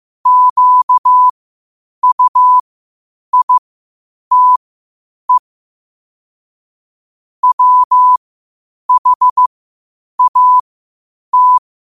Morse code, heard as a keyed tone, about 15 wpm, spaced as for 8 wpm, 1000 Hz. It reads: QUITE WHAT
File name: qotdfiles / 8wpm